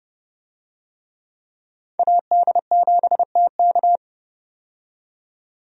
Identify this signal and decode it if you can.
Morse code, heard as a keyed tone, about 30 wpm, 700 Hz. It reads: AD7TK
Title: USA Call Signs - 5 Characters - 30wpm